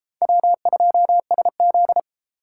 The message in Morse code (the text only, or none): W2SZ